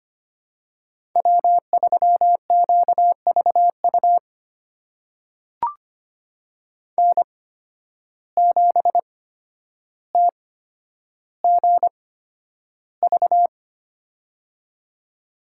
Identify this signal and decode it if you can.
Morse code, heard as a keyed tone, about 25 wpm, spaced as for 11 wpm, 700 Hz. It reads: W3QVU N7TGV
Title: USA Call Signs - 5 Characters - 8x Character Spacing - 25wpm